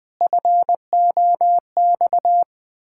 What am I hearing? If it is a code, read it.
Morse code, heard as a keyed tone, about 20 wpm, 700 Hz. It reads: FOX
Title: Pangram Challenge v2 - 20wpm